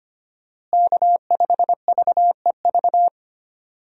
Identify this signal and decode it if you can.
Morse code, heard as a keyed tone, about 25 wpm, 700 Hz. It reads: K5VEV
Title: USA Call Signs - 5 Characters - Repeated 3x - 25wpm